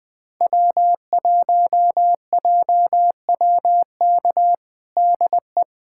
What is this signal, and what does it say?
Morse code, heard as a keyed tone, about 20 wpm, 700 Hz. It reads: W1JWK DE
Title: USA Callsign DE Callsign - 5 Characters - 20wpm